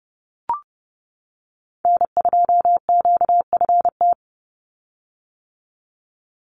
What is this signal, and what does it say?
Morse code, heard as a keyed tone, about 29 wpm, 700 Hz. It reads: N2QFT
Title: USA Call Signs 30wpm